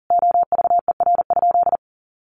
Morse code, heard as a keyed tone, about 40 wpm, 700 Hz. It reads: OVER?
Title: Sentences from Top 1200 Words 40wpm